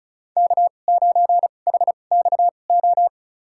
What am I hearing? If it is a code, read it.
Morse code, heard as a keyed tone, about 34 wpm, spaced as for 25 wpm, 700 Hz. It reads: K9HXO
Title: USA Call Signs - 5 Characters - 2x Character Spacing - 35wpm